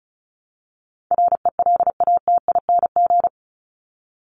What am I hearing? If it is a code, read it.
Morse code, heard as a keyed tone, about 34 wpm, 700 Hz. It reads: RELATING
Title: ING Suffix - Top 100 Words with Base Words 35wpm